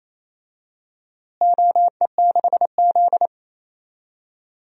Morse code, heard as a keyed tone, about 28 wpm, 700 Hz. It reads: OE6Z
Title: DXCC-100 Call Signs - 4 Characters - Repeated 2x - 28wpm